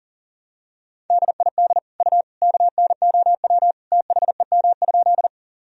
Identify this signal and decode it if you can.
Morse code, heard as a keyed tone, about 40 wpm, 700 Hz. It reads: DID U KNOW THEM?
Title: Sentences from Top 100 Words and Abbreviations 40wpm